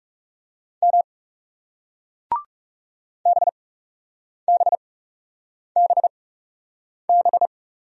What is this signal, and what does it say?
Morse code, heard as a keyed tone, about 38 wpm, spaced as for 8 wpm, 700 Hz. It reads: M BBBB